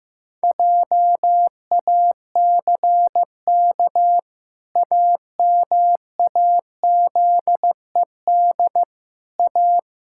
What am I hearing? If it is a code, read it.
Morse code, heard as a keyed tone, about 16 wpm, 700 Hz. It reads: JACK AMAZED A
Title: Pangram Challenge v2 - 15wpm